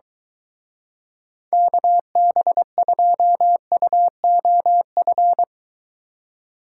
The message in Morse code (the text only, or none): KB2UOF